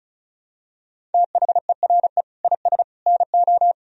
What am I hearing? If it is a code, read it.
Morse code, heard as a keyed tone, about 34 wpm, 700 Hz. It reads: THERE IS NO
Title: Sentences from Top 1000 Words - Review Part 1 - 35wpm